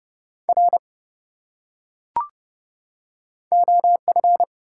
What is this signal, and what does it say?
Morse code, heard as a keyed tone, about 26 wpm, 700 Hz. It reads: R OF